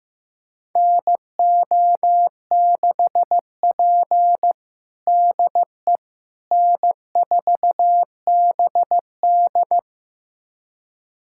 Morse code, heard as a keyed tone, about 15 wpm, 700 Hz. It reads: NO6P DE N4BD
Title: USA Callsign DE Callsign - 4 Characters - 15wpm